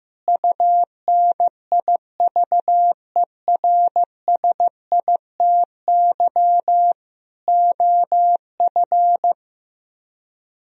Morse code, heard as a keyed tone, about 16 wpm, 700 Hz. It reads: UNIVERSITY OF
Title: Speed-Racing - Sets of 4 Words 15wpm